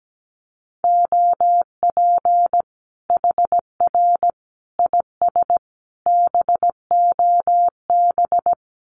Morse code, heard as a keyed tone, about 17 wpm, 700 Hz. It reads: OP HR IS BOB
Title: Common QSO Phrases 17wpm